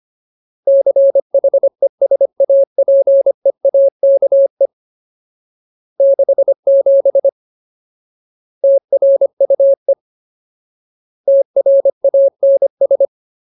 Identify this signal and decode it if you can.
Morse code, heard as a keyed tone, about 25 wpm, 550 Hz. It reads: CHESAPEAKE 67 TRUE TRANS
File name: Ultimate Word List / 25WPM Ultimate